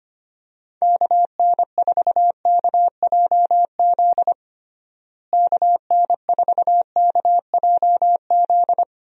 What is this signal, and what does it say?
Morse code, heard as a keyed tone, about 25 wpm, 700 Hz. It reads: KN4KJZ KN4KJZ
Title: USA Call Signs - 6 Characters - Repeated 2x - 25wpm